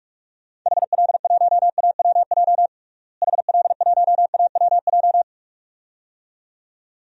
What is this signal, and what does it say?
Morse code, heard as a keyed tone, about 47 wpm, 700 Hz. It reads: HL1AWJ HL1AWJ
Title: DXCC-100 Call Signs - 6 Characters - Repeated 2x - 45wpm